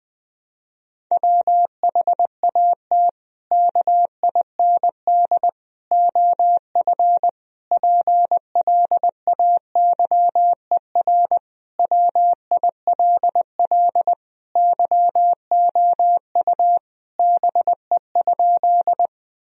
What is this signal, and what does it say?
Morse code, heard as a keyed tone, about 19 wpm, 700 Hz. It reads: WHAT KIND OF PLAYER WILL YOU BE?
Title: Sentences from Top 1000 Words - Review Part 4 - 20wpm